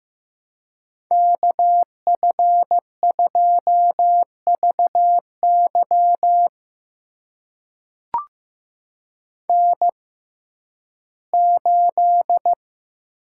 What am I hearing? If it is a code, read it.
Morse code, heard as a keyed tone, about 15 wpm, spaced as for 11 wpm, 700 Hz. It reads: KF2VY N8
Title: USA Call Signs - 5 Characters - 6x Character Spacing - 15wpm